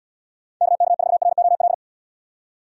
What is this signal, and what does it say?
Morse code, heard as a keyed tone, about 69 wpm, 700 Hz. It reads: KK4ROL